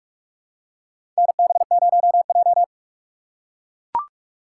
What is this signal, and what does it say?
Morse code, heard as a keyed tone, about 43 wpm, 700 Hz. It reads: NB0J
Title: USA Call Signs - 4 Characters - 3x Character Spacing - 45wpm